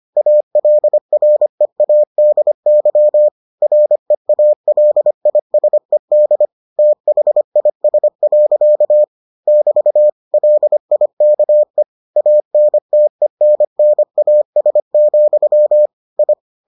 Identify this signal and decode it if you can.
Morse code, heard as a keyed tone, about 25 wpm, 600 Hz. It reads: ALREADY REALISED THIS. <BT> LIKE ANTENNAS, I